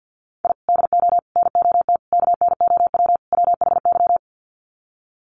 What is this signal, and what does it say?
Morse code, heard as a keyed tone, about 50 wpm, 700 Hz. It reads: I DO NOT KNOW WHY
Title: Sets of 5 Words 50wpm